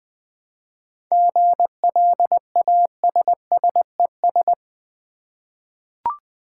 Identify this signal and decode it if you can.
Morse code, heard as a keyed tone, about 20 wpm, 700 Hz. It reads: GLASSES
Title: Easy Words with Hints - 20wpm